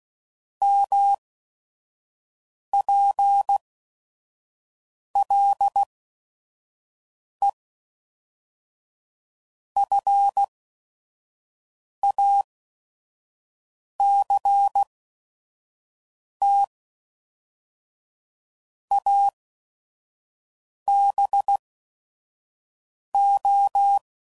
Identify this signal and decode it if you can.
Morse code, heard as a keyed tone, about 16 wpm, spaced as for 5 wpm, 800 Hz. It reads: MPLE FACT ABO
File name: CW Code Groups @5WPM / Technical Subjects / OR Gate